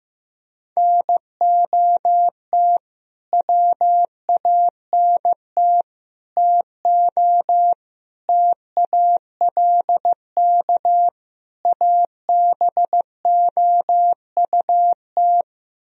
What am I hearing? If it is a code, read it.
Morse code, heard as a keyed tone, about 16 wpm, 700 Hz. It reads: NOT WANT TO TALK ABOUT
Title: Speed-Racing - Sets of 5 Words 15wpm